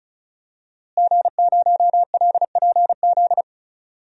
Morse code, heard as a keyed tone, about 36 wpm, 700 Hz. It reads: G0LPZ
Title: DXCC-100 Call Signs - 5 Characters - Repeated 3x - 35wpm